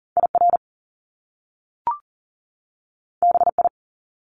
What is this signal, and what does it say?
Morse code, heard as a keyed tone, about 40 wpm, 700 Hz. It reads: IR BI
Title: Top Two Letter Combinations in Words 40wpm